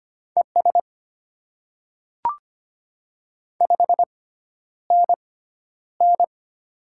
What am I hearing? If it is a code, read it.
Morse code, heard as a keyed tone, about 25 wpm, spaced as for 11 wpm, 700 Hz. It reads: ES 5NN
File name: Top 20 QSO Elements - 6x Character Spacing - 25wpm